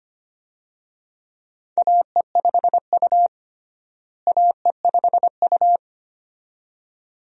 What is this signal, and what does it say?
Morse code, heard as a keyed tone, about 25 wpm, 700 Hz. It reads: AE5U AE5U
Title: USA Call Signs - 4 Characters - Repeated 2x - 25wpm